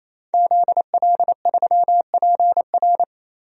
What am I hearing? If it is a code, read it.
Morse code, heard as a keyed tone, about 28 wpm, 700 Hz. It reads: ZL3PR
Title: DXCC-100 Call Signs - 5 Characters - 28wpm